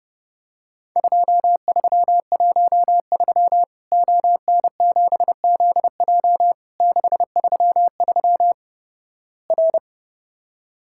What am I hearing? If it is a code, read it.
Morse code, heard as a keyed tone, about 30 wpm, 700 Hz. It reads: 2313 ON7ZJ 633 R
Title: Worked All Europe DX Contest - QTC Exchange - 30wpm